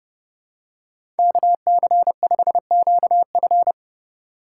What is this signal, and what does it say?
Morse code, heard as a keyed tone, about 30 wpm, 700 Hz. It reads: KC5QF